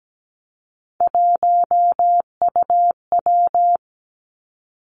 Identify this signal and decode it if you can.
Morse code, heard as a keyed tone, about 17 wpm, 700 Hz. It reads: 1UW
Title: Three Challenging Letter-Number Combinations 17wpm